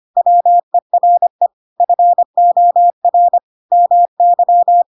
Morse code, heard as a keyed tone, about 25 wpm, 700 Hz. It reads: WERE FOR MY
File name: Pinocchio / Pinocchio ch7 25 25 ik1hge